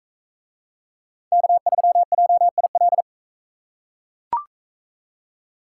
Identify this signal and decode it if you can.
Morse code, heard as a keyed tone, about 40 wpm, 700 Hz. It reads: K3JIL